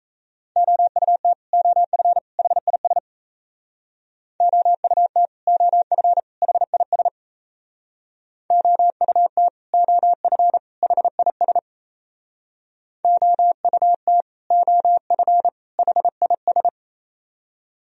Morse code, heard as a keyed tone, about 36 wpm, 700 Hz. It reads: OUT OF HIS OUT OF HIS OUT OF HIS OUT OF HIS